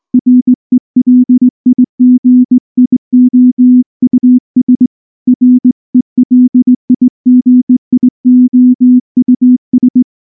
Morse code, heard as a keyed tone, about 19 wpm, 250 Hz. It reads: RELIGIOUS RELIGIOUS
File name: morse / morse-training-20WPM-10